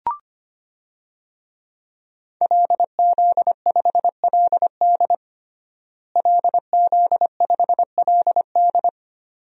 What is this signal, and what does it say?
Morse code, heard as a keyed tone, about 25 wpm, 700 Hz. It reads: LZ5LD LZ5LD
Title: DXCC-100 Call Signs - 5 Characters - Repeated 2x - 25wpm